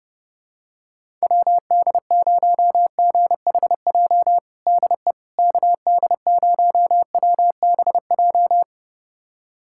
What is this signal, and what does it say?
Morse code, heard as a keyed tone, about 30 wpm, 700 Hz. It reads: WD0GHJ DE KD0WBJ
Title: USA Callsign DE Callsign - 6 Characters - 30wpm